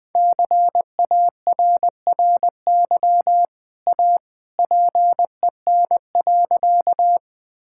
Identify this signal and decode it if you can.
Morse code, heard as a keyed tone, about 21 wpm, 700 Hz. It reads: CARRY A PEN.
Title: Easy Sentences with Hints - 20wpm